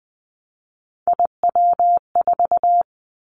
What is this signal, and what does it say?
Morse code, heard as a keyed tone, about 20 wpm, 700 Hz. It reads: IW4
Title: Callsign Prefixes for Top 50 DXCC - 20wpm